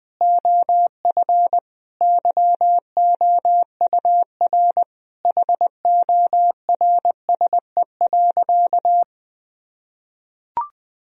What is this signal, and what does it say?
Morse code, heard as a keyed tone, about 20 wpm, 700 Hz. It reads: OF YOUR HORSE.
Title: Sentences from Top 1000 Words - Review Part 2 - 20wpm